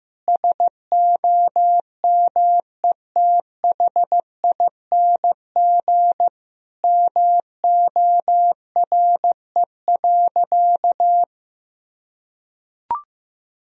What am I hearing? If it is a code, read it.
Morse code, heard as a keyed tone, about 16 wpm, 700 Hz. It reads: SOMETHING MORE.
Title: Sentences from Top 1000 Words - Review Part 1 - 15wpm